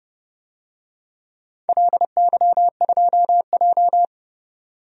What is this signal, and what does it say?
Morse code, heard as a keyed tone, about 30 wpm, 700 Hz. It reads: LY2J
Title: DXCC-100 Call Signs - 4 Characters - 30wpm